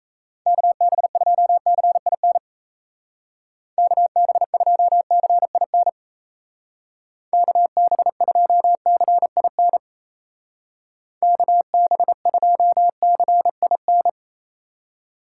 Morse code, heard as a keyed tone, about 33 wpm, 700 Hz. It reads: KB2CIN KB2CIN KB2CIN KB2CIN
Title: Speed Racing - USA 2x3 Call Signs 28wpm